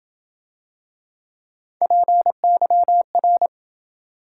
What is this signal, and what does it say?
Morse code, heard as a keyed tone, about 27 wpm, 700 Hz. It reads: PYR